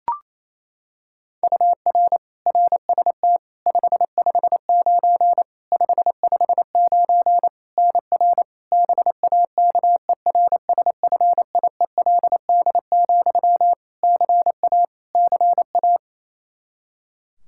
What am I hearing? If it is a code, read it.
Morse code, heard as a keyed tone, about 28 wpm, 700 Hz. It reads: UR RST 559 559 NR BAKERSFIELD, CA CA
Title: Practice QSO - Exchange 1 - Beginning - 28wpm